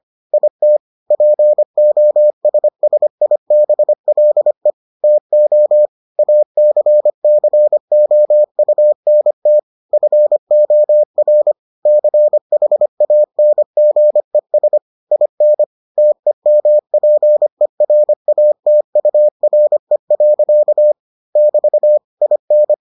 Morse code, heard as a keyed tone, about 25 wpm, 600 Hz. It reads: IT POSSIBLE TO ACCOUNT FOR CHANGES IN TEMPERATURE. <BT> IN